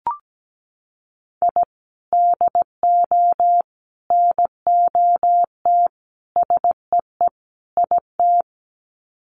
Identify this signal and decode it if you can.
Morse code, heard as a keyed tone, about 17 wpm, 700 Hz. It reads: I DO NOT SEE IT
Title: Sets of 5 Words Encore2 17wpm